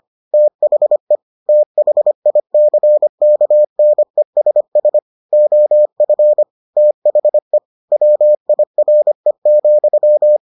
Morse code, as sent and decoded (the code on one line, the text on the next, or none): - .... / - .... .. -.-. -.- -. ... ... / --- ..-. / - .... / .-- .. .-. --..--
THE THICKNESS OF THE WIRE,